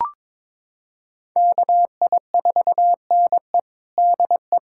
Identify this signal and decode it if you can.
Morse code, heard as a keyed tone, about 22 wpm, 700 Hz. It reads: KI4NE DE
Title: USA Callsign DE Callsign 22wpm